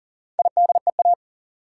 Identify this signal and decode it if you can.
Morse code, heard as a keyed tone, about 40 wpm, 700 Hz. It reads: IDEA
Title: Sentences from Top 300 Words - New Word Drill - 40wpm